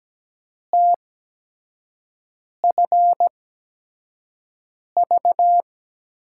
Morse code, heard as a keyed tone, about 17 wpm, spaced as for 5 wpm, 700 Hz. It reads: TFV
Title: USA Call Signs - 8x Character Spacing - 17wpm